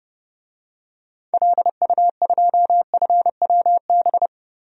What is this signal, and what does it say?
Morse code, heard as a keyed tone, about 30 wpm, 700 Hz. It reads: LU2FWB
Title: DXCC-100 Call Signs - 6 Characters - Repeated 2x - 30wpm